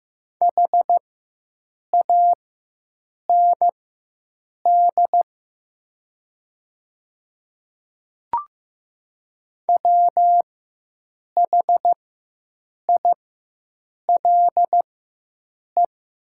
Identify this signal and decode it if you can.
Morse code, heard as a keyed tone, about 15 wpm, spaced as for 7 wpm, 700 Hz. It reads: HAND WHILE
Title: 4x Character Spacing - Top 300 Words - 15wpm